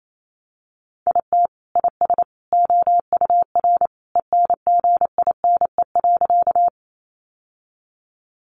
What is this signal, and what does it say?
Morse code, heard as a keyed tone, about 28 wpm, 700 Hz. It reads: IT IS OUR ENGINE.
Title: Sentences from Top 1500 Words 28wpm